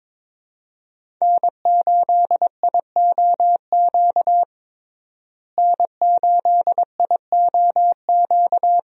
Morse code, heard as a keyed tone, about 22 wpm, 700 Hz. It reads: N8IOQ N8IOQ
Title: USA Call Signs - 5 Characters - Repeated 2x - 22wpm